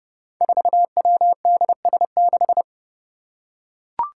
Morse code, heard as a keyed tone, about 30 wpm, 700 Hz. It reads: 4WDS6
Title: Five Challenging Letter-Number Combinations 30wpm